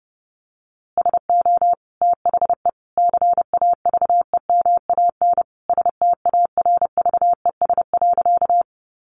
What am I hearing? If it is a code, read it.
Morse code, heard as a keyed tone, about 30 wpm, 700 Hz. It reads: SO THE CAVEMAN STARVES.
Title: A Sound Of Thunder 30wpm